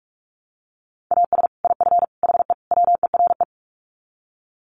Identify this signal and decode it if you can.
Morse code, heard as a keyed tone, about 44 wpm, 700 Hz. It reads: AS IF HE WERE
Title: Sets of 4 Words 45wpm